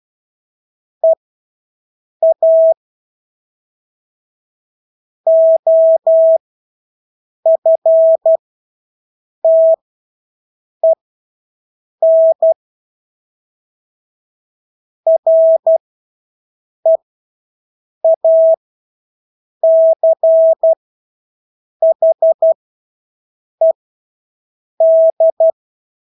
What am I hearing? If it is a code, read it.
Morse code, heard as a keyed tone, about 12 wpm, spaced as for 6 wpm, 650 Hz. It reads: EA OFTEN REACHED
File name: Engelske ord 12-6 wpm